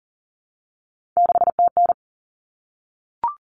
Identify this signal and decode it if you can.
Morse code, heard as a keyed tone, about 40 wpm, 700 Hz. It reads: 6TN
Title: Common Cut Numbers - 3-Digit 40wpm